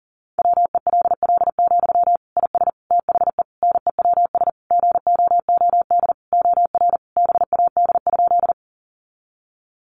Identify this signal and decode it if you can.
Morse code, heard as a keyed tone, about 37 wpm, 700 Hz. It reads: WELL, IS THE NEWS GOOD OR BAD?
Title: Sentences from Top 800 Words 40wpm